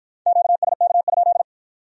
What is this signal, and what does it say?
Morse code, heard as a keyed tone, about 53 wpm, 700 Hz. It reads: QSK?